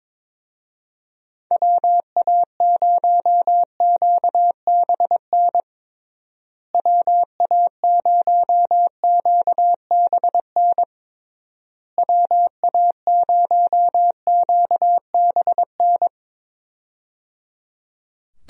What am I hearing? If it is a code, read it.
Morse code, heard as a keyed tone, about 22 wpm, 700 Hz. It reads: WA0QBN WA0QBN WA0QBN
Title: USA Call Signs - 6 Characters - Repeated 3x - 22wpm